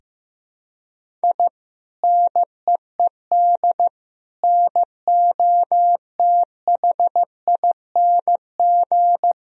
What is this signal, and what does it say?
Morse code, heard as a keyed tone, about 15 wpm, 700 Hz. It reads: I NEED NOTHING
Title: Sentences from Top 1000 Words - Review Part 1 - 15wpm